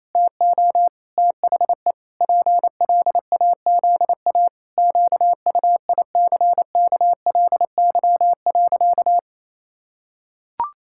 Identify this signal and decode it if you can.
Morse code, heard as a keyed tone, about 28 wpm, 700 Hz. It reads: TO THE PLAZA QUICKLY.
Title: Pangram Challenge v2 - 28wpm